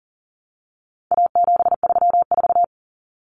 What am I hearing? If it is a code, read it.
Morse code, heard as a keyed tone, about 40 wpm, 700 Hz. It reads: A734
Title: Common Cut Numbers - 4-Digit 40wpm